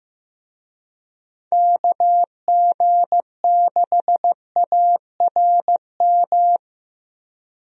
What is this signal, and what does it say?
Morse code, heard as a keyed tone, about 15 wpm, 700 Hz. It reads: KG6ARM
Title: USA Call Signs - 6 Characters - Repeated 2x - 15wpm